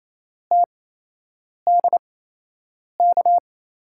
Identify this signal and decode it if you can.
Morse code, heard as a keyed tone, about 28 wpm, spaced as for 8 wpm, 700 Hz. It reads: TDK
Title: USA Call Signs - 8x Character Spacing - 28wpm